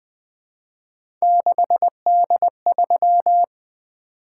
Morse code, heard as a keyed tone, about 20 wpm, 700 Hz. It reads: 6D3
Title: Three Letter-Number Combinations - 20wpm